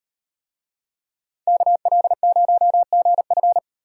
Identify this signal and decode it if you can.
Morse code, heard as a keyed tone, about 36 wpm, 700 Hz. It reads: KL0GF